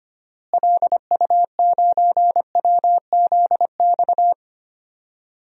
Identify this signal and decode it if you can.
Morse code, heard as a keyed tone, about 25 wpm, 700 Hz. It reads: LU9WZX
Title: DXCC-100 Call Signs - 6 Characters - Repeated 3x - 25wpm